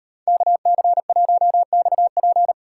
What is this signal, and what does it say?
Morse code, heard as a keyed tone, about 30 wpm, 700 Hz. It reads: KC1XP
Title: Speed Racing - USA 2x2 Call Signs 25wpm